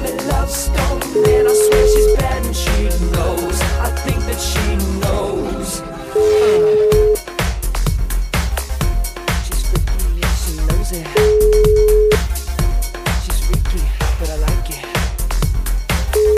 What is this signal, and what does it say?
Ringback tone (Europe, 425 Hz), with speech.